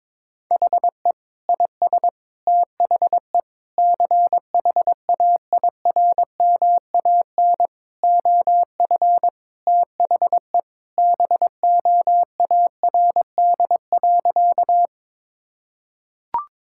Morse code, heard as a keyed tone, about 22 wpm, 700 Hz. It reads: HE IS THE CHAIRMAN OF THE BOARD.
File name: practice / 22wpm / Sentences with 3-Word Phrases - 22wpm